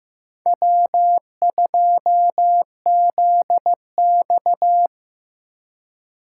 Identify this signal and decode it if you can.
Morse code, heard as a keyed tone, about 15 wpm, 700 Hz. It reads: W 2 Z X
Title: USA Call Signs - 4 Characters - 6x Character Spacing - 15wpm